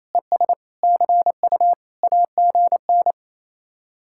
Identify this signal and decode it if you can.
Morse code, heard as a keyed tone, about 38 wpm, 700 Hz. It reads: ES CU AGN